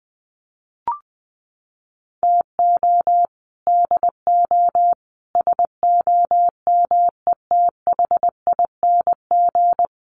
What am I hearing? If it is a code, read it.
Morse code, heard as a keyed tone, about 20 wpm, 700 Hz. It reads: TO DO SOMETHING
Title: Sets of 5 Words 20wpm